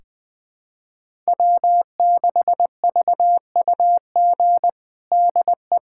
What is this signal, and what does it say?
Morse code, heard as a keyed tone, about 20 wpm, 700 Hz. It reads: W6VUG DE
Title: USA Callsign DE Callsign - 5 Characters - 20wpm